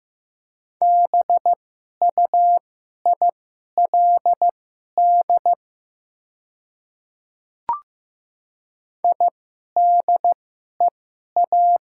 Morse code, heard as a keyed tone, about 15 wpm, spaced as for 11 wpm, 700 Hz. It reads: BUILD IDEA
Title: 2x Character Spacing - Top 300 Words - 15wpm